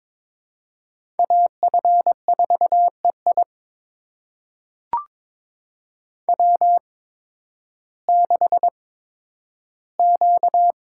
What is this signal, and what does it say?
Morse code, heard as a keyed tone, about 23 wpm, spaced as for 12 wpm, 700 Hz. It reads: AF4EI W6Q
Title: USA Call Signs - 5 Characters - 8x Character Spacing - 22wpm